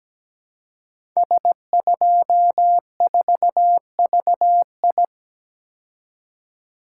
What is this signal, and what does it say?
Morse code, heard as a keyed tone, about 17 wpm, 700 Hz. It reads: S24VI
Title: Code Groups - 5 Letter:Numbers - 17wpm